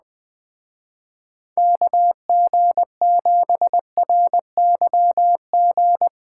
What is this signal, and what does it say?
Morse code, heard as a keyed tone, about 20 wpm, 700 Hz. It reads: KG7RYG
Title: Speed Racing - USA Call Signs 17wpm